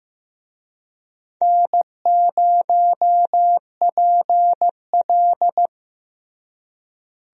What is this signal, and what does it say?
Morse code, heard as a keyed tone, about 15 wpm, 700 Hz. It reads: N0PL
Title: USA Call Signs - 4 Characters - Repeated 2x - 15wpm